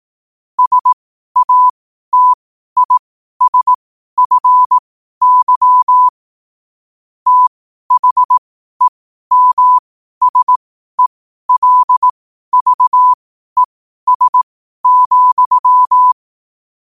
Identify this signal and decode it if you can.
Morse code, heard as a keyed tone, about 18 wpm, spaced as for 12 wpm, 1000 Hz. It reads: SATISFY THEMSELVES,
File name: qotdfiles / 12wpm